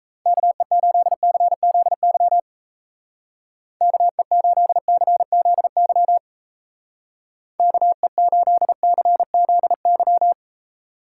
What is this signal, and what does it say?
Morse code, heard as a keyed tone, about 34 wpm, 700 Hz. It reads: KE8CZY KE8CZY KE8CZY
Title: Speed Racing - USA 2x3 Call Signs 28wpm